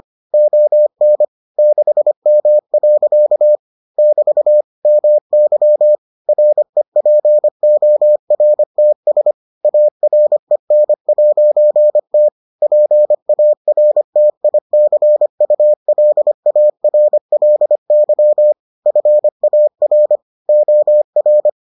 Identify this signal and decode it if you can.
Morse code, heard as a keyed tone, about 25 wpm, 600 Hz. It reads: ON 6M. <BT> MY REPORTS AREN'T PARTICULARLY FAR OR